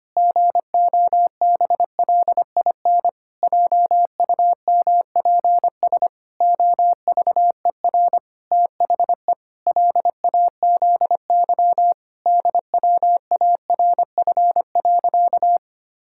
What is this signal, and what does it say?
Morse code, heard as a keyed tone, about 25 wpm, 700 Hz. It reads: GOBLIN JUMPS OVER THE LAZY DWARF.
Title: Pangram Challenge v2 - 25wpm